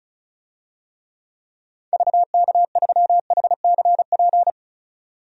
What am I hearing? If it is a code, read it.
Morse code, heard as a keyed tone, about 36 wpm, 700 Hz. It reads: VK3HCP
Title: DXCC-100 Call Signs - 6 Characters - 35wpm